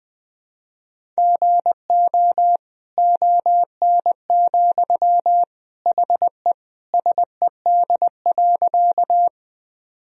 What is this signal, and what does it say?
Morse code, heard as a keyed tone, about 20 wpm, 700 Hz. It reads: GO ON, HE SED.
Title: Sentences from Top 100 Words and Abbreviations 20wpm